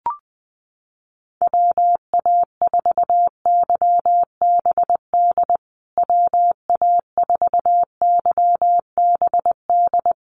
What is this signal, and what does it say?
Morse code, heard as a keyed tone, about 20 wpm, 700 Hz. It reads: WA4YBD WA4YBD
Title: USA Call Signs Repeated 3x 20wpm